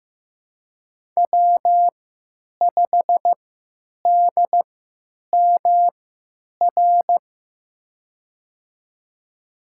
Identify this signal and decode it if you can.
Morse code, heard as a keyed tone, about 15 wpm, spaced as for 9 wpm, 700 Hz. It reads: W5DMR
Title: USA Call Signs - 5 Characters - 3x Character Spacing - 15wpm